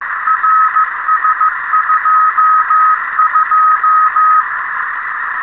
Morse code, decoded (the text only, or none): RS22